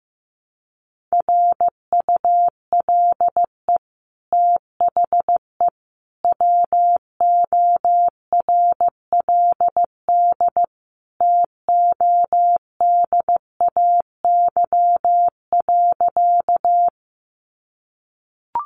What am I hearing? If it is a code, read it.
Morse code, heard as a keyed tone, about 15 wpm, 700 Hz. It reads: RULE THE WORLD TODAY.
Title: Sentences from Top 600 Words 15wpm